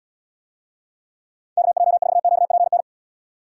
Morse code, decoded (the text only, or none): KJ4QPN